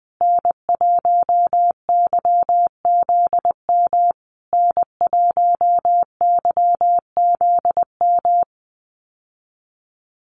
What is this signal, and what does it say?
Morse code, heard as a keyed tone, about 20 wpm, 700 Hz. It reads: N1YZM N1YZM
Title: USA Call Signs Repeated 3x 20wpm